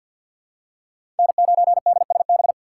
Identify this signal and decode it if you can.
Morse code, heard as a keyed tone, about 47 wpm, 700 Hz. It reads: N9DSB
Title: USA Call Signs - 5 Characters - 6x Character Spacing - 50wpm